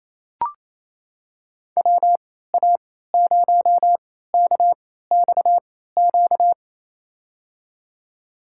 Morse code, heard as a keyed tone, about 28 wpm, spaced as for 16 wpm, 700 Hz. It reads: WA0KXQ
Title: USA Call Signs - 6 Characters - 3x Character Spacing - 28wpm